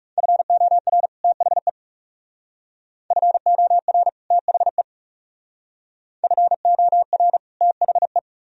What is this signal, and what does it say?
Morse code, heard as a keyed tone, about 34 wpm, 700 Hz. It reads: FOR THE FOR THE FOR THE
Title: Speed-Racing - Sets of 2 Words 30wpm